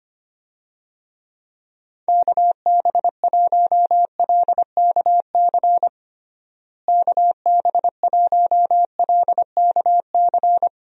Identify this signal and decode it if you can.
Morse code, heard as a keyed tone, about 25 wpm, 700 Hz. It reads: KB1LKC KB1LKC
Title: USA Call Signs - 6 Characters - Repeated 2x - 25wpm